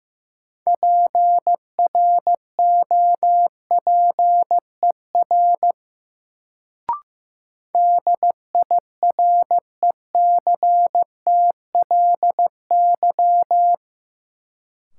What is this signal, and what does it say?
Morse code, heard as a keyed tone, about 16 wpm, 700 Hz. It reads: PROPER DIRECTLY
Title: Sentences from Top 800 Words - New Word Drill - 15wpm